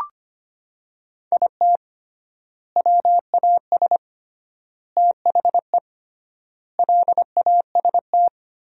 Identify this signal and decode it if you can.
Morse code, heard as a keyed tone, about 25 wpm, 700 Hz. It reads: IT WAS THE LAST
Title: 3x Word-Spacing - Sentences from Top 300 Words - 25wpm